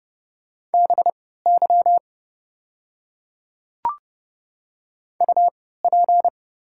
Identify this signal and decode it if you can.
Morse code, heard as a keyed tone, about 30 wpm, spaced as for 17 wpm, 700 Hz. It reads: BY UP
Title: 3x Character Spacing - Top 200 Words - 30wpm